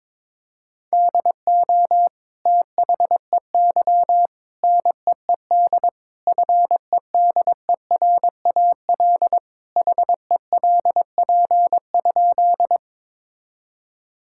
Morse code, heard as a keyed tone, about 22 wpm, 700 Hz. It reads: DO THEY NEED FEDERAL HELP?
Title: Sentences from Top 1000 Words - Review Part 5 - 22wpm